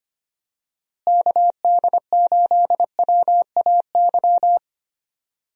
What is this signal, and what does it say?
Morse code, heard as a keyed tone, about 25 wpm, 700 Hz. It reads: KD8WAY
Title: USA Call Signs - 6 Characters - Repeated 2x - 25wpm